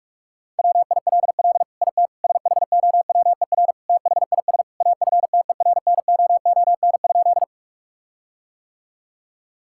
Morse code, heard as a keyed tone, about 46 wpm, 700 Hz. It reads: WILL IT SHOWER THIS AFTERNOON?
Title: Sentences from Top 1600 Words 45wpm